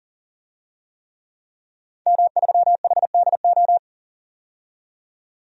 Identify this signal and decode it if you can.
Morse code, heard as a keyed tone, about 40 wpm, 700 Hz. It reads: M3HDO
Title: DXCC-100 Call Signs - 5 Characters - 40wpm